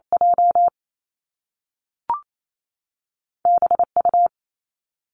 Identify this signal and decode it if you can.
Morse code, heard as a keyed tone, about 28 wpm, 700 Hz. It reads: J B U
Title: Two Challenging Letter-Number Combinations 28wpm